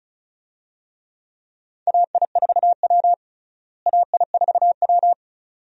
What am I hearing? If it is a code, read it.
Morse code, heard as a keyed tone, about 36 wpm, 700 Hz. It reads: AI4W AI4W
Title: USA Call Signs - 4 Characters - Repeated 2x - 35wpm